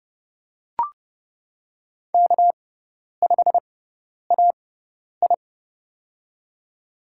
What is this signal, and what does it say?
Morse code, heard as a keyed tone, about 29 wpm, spaced as for 10 wpm, 700 Hz. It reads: K5AI
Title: USA Call Signs - 4 Characters - 6x Character Spacing - 30wpm